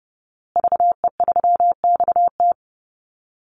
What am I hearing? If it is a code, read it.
Morse code, heard as a keyed tone, about 30 wpm, 700 Hz. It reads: VE3XT
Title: Call Signs Repeated 2x - 30wpm